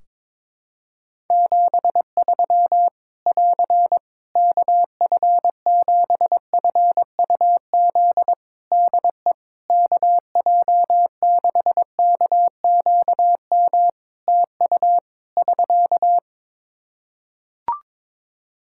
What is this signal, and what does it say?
Morse code, heard as a keyed tone, about 21 wpm, 700 Hz. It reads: 73 <AR> KF7FUZ DE KJ6KQM TU <SK>
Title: Practice QSO - Exchange 4 - Ending - 22wpm